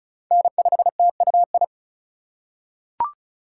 Morse code, heard as a keyed tone, about 36 wpm, 700 Hz. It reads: N5TUI
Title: USA Call Signs - 2x Character Spacing - 35wpm